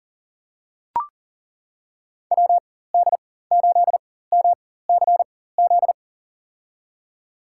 Morse code, heard as a keyed tone, about 40 wpm, spaced as for 19 wpm, 700 Hz. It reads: WD8MCZ
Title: USA Call Signs - 6 Characters - 4x Character Spacing - 40wpm